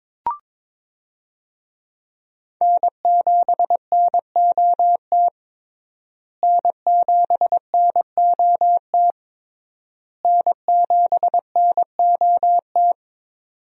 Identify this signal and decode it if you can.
Morse code, heard as a keyed tone, about 22 wpm, 700 Hz. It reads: N7NOT N7NOT N7NOT
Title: USA Call Signs - 5 Characters - Repeated 3x - 22wpm